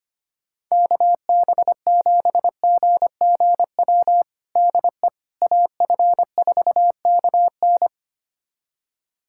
Morse code, heard as a keyed tone, about 25 wpm, 700 Hz. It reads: KB7GGW DE AF4KN
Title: USA Callsign DE Callsign 25wpm